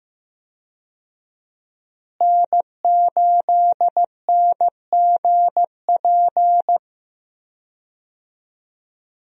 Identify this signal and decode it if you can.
Morse code, heard as a keyed tone, about 15 wpm, 700 Hz. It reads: N8NGP
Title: USA Call Signs - 5 Characters - Repeated 2x - 15wpm